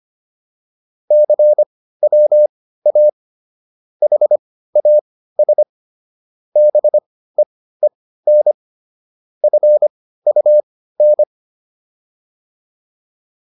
Morse code, heard as a keyed tone, about 25 wpm, spaced as for 15 wpm, 600 Hz. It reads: CWA HAS BEEN FUN
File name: PR305 15